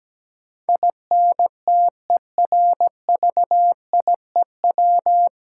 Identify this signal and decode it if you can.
Morse code, heard as a keyed tone, about 17 wpm, 700 Hz. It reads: INTERVIEW
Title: Sentences from Top 1600 Words - New Word Drill - 17wpm